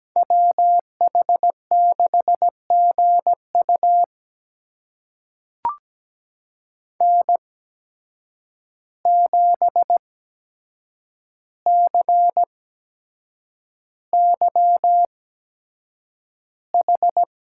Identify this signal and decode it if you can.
Morse code, heard as a keyed tone, about 17 wpm, spaced as for 7 wpm, 700 Hz. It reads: WH6GU N7CYH
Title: USA Call Signs - 5 Characters - 8x Character Spacing - 17wpm